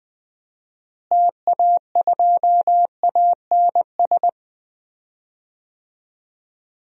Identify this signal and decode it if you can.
Morse code, heard as a keyed tone, about 20 wpm, 700 Hz. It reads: TA2ANS
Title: DXCC-100 Call Signs - 6 Characters - Repeated 3x - 20wpm